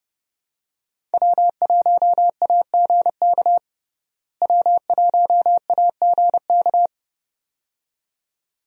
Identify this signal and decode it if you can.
Morse code, heard as a keyed tone, about 31 wpm, 700 Hz. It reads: W1AGK W1AGK
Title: USA Call Signs - 5 Characters - Repeated 2x - 30wpm